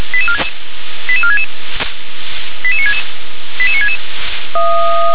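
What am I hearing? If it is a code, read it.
Five-tone selective call (ZVEI), heard as quick runs of five tones, tones of about 80 ms. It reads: 88466 88966 88966
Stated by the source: ZVEI five-tone call 88965 received via SDR.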